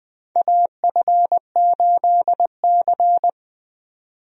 Morse code, heard as a keyed tone, about 20 wpm, 700 Hz. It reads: AF8C
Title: USA Call Signs - 4 Characters - 20wpm